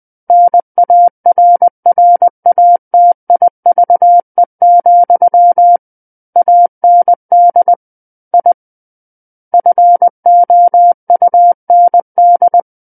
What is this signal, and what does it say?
Morse code, heard as a keyed tone, about 20 wpm, 700 Hz. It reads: NARRATIVE, AND I FOUND
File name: 20 WPM